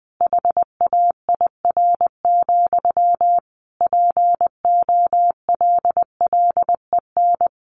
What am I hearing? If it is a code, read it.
Morse code, heard as a keyed tone, about 20 wpm, 700 Hz. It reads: HAIR, POLLEN
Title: A Sound Of Thunder 20wpm